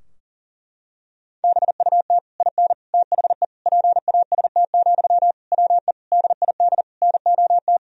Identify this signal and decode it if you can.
Morse code, heard as a keyed tone, about 37 wpm, 700 Hz. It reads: BUT IN THE PAST, WE DID NOT
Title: Sentences from Top 1000 Words - Review Part 3 - 40wpm